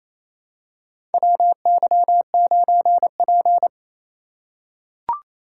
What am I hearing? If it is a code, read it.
Morse code, heard as a keyed tone, about 28 wpm, 700 Hz. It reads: WY9P
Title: USA Call Signs - 4 Characters - 4x Character Spacing - 28wpm